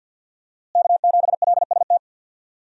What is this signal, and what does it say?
Morse code, heard as a keyed tone, about 50 wpm, 700 Hz. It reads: K7LST
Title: USA Call Signs - 5 Characters - Repeated 2x - 50wpm